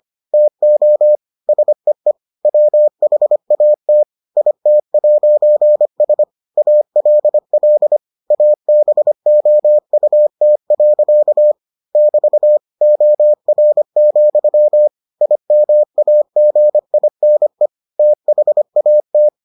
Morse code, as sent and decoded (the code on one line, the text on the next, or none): - --- / ... . / .-- .... .- - / .. - .----. ... / .- .-.. .-.. / .- -... --- ..- - .-.-.- / -...- / --- .-. --..-- / .. -- .- --. .. -. / - .... .- -
TO SEE WHAT IT'S ALL ABOUT. <BT> OR, IMAGINE THAT